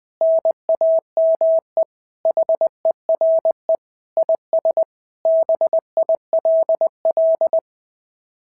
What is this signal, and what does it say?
Morse code, heard as a keyed tone, about 19 wpm, 650 Hz. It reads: NAME HERE IS BILL
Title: SOTA QSO Practice 20wpm